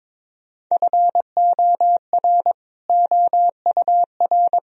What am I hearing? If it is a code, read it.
Morse code, heard as a keyed tone, about 22 wpm, 700 Hz. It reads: FOR OUR
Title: Sentences from Top 1000 Words - Review Part 3 - 22wpm